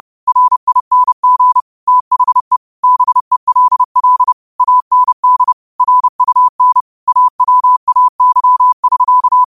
Morse code, heard as a keyed tone, about 30 wpm, 1000 Hz. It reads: RING THE BELL AND RUN AWAY<SK>
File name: qotdfiles / 30wpm